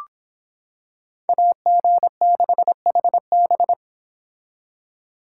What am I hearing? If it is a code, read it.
Morse code, heard as a keyed tone, about 26 wpm, 700 Hz. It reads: AG6HB